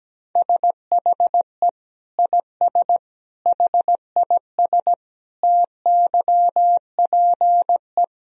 Morse code, heard as a keyed tone, about 17 wpm, 700 Hz. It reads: SHE IS HIS TYPE
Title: Sentences from Top 1000 Words - Review Part 3 - 17wpm